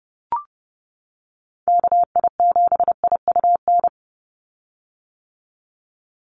Morse code, heard as a keyed tone, about 30 wpm, 700 Hz. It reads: KI7IUN
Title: USA Call Signs 30wpm